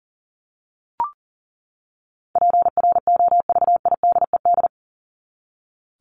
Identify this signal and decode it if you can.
Morse code, heard as a keyed tone, about 40 wpm, 700 Hz. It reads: PROVIDED
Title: Top 1751-2000 Words 40wpm